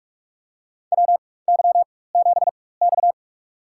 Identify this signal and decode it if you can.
Morse code, heard as a keyed tone, about 46 wpm, spaced as for 21 wpm, 700 Hz. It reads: WY7X
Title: USA Call Signs - 4 Characters - 4x Character Spacing - 45wpm